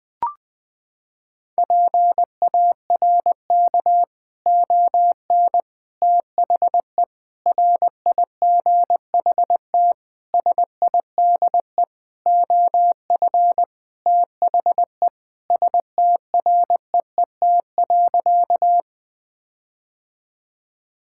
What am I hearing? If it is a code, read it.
Morse code, heard as a keyed tone, about 20 wpm, 700 Hz. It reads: PARK ON THE RIGHT SIDE OF THE STREET.
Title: Sentences with 6-Word Phrases - 20wpm